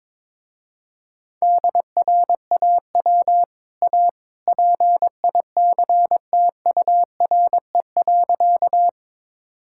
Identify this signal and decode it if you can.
Morse code, heard as a keyed tone, about 21 wpm, 700 Hz. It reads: DRAW A PICTURE.
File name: practice / 22wpm / Easy Sentences with Hints - 22wpm